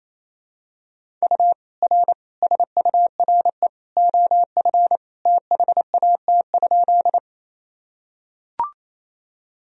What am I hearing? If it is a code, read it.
Morse code, heard as a keyed tone, about 28 wpm, 700 Hz. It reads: U R SURE OF THAT?
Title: Sentences from Top 300 Words and Abbreviations 28wpm